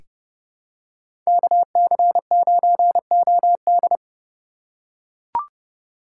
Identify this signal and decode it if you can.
Morse code, heard as a keyed tone, about 30 wpm, 700 Hz. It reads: KC9OD
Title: USA Call Signs - 5 Characters - 3x Character Spacing - 30wpm